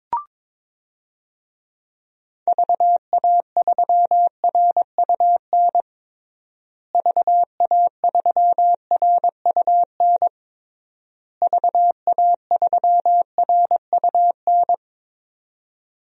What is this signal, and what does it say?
Morse code, heard as a keyed tone, about 22 wpm, 700 Hz. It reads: VA3RUN VA3RUN VA3RUN
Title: DXCC-100 Call Signs - 6 Characters - Repeated 3x - 22wpm